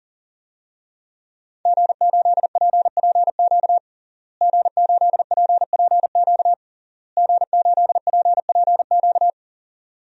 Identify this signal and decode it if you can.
Morse code, heard as a keyed tone, about 40 wpm, 700 Hz. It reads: G8PPQ G8PPQ G8PPQ
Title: DXCC-100 Call Signs - 5 Characters - Repeated 3x - 40wpm